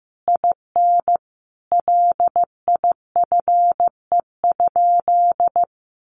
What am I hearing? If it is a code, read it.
Morse code, heard as a keyed tone, about 15 wpm, 700 Hz. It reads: IN LIFE?
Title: Sentences from Top 1400 Words 15wpm